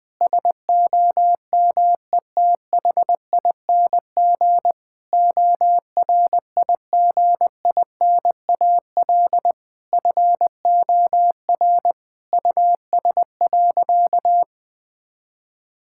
Morse code, heard as a keyed tone, about 20 wpm, 700 Hz. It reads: SOMETHING ORIGINAL FOR US.
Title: Sentences from Top 1000 Words - Review Part 6 - 20wpm